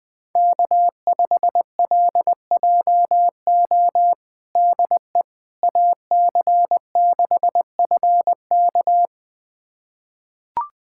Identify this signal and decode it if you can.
Morse code, heard as a keyed tone, about 20 wpm, 700 Hz. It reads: K5LJO DE AC6FK
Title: USA Callsign DE Callsign - 5 Characters - 20wpm